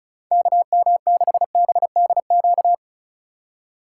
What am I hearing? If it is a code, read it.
Morse code, heard as a keyed tone, about 36 wpm, 700 Hz. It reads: KM6BDQ
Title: USA Call Signs - 6 Characters - 2x Character Spacing - 35wpm